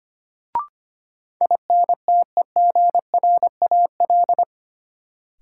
Morse code, heard as a keyed tone, about 25 wpm, 700 Hz. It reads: INTEGRAL
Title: Sets of 5 Words - New Word Drill - 25wpm